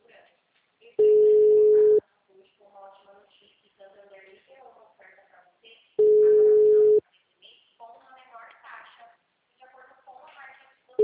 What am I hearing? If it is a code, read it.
Ringback tone (Europe, 425 Hz), with speech.